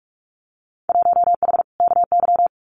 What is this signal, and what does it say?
Morse code, heard as a keyed tone, about 46 wpm, 700 Hz. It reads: 1H KY
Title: Winter Field Day Exchanges 45wpm